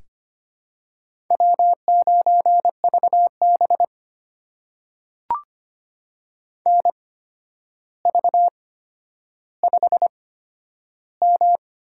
Morse code, heard as a keyed tone, about 25 wpm, spaced as for 11 wpm, 700 Hz. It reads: W9VB NV5M
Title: USA Call Signs - 4 Characters - 8x Character Spacing - 25wpm